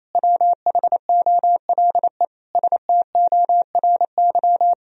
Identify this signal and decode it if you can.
Morse code, heard as a keyed tone, about 28 wpm, 700 Hz. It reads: WHOLE STORY
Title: Sentences from Top 1000 Words - Review Part 4 - 28wpm